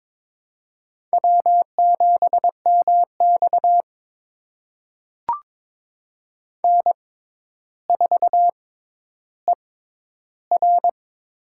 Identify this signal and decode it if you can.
Morse code, heard as a keyed tone, about 23 wpm, spaced as for 11 wpm, 700 Hz. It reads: W7MX N4ER
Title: USA Call Signs - 4 Characters - 6x Character Spacing - 22wpm